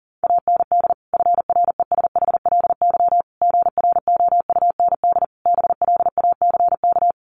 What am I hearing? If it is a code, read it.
Morse code, heard as a keyed tone, about 40 wpm, 700 Hz. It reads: AND FRESHLY GROUND BLACK
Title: Sets of 5 Words Encore 40wpm